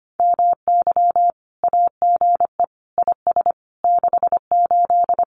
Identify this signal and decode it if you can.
Morse code, heard as a keyed tone, about 25 wpm, 700 Hz. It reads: MY AGE IS 68
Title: Common QSO Phrases Encore 25wpm